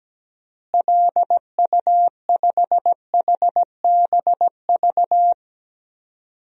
Morse code, heard as a keyed tone, about 17 wpm, 700 Hz. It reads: LU5HBV
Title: DXCC-100 Call Signs - 6 Characters - Repeated 2x - 17wpm